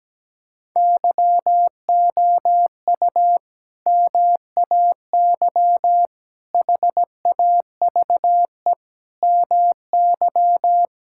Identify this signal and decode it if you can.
Morse code, heard as a keyed tone, about 17 wpm, 700 Hz. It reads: YOU MAY HAVE MY
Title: Sentences from Top 1000 Words - Review Part 6 - 17wpm